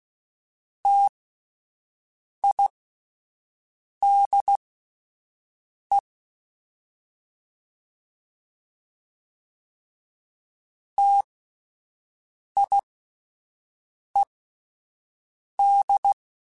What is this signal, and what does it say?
Morse code, heard as a keyed tone, about 16 wpm, spaced as for 5 wpm, 800 Hz. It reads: TIDE TIED